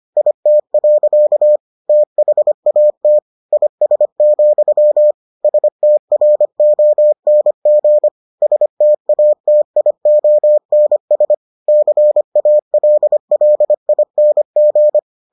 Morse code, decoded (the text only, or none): IT. THAT IS, STRONG STATIONS CALLING